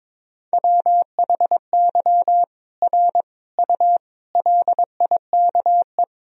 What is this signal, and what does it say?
Morse code, heard as a keyed tone, about 22 wpm, 700 Hz. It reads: WHY R U LIKE
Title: Sentences from Top 200 Words and Abbreviations 22wpm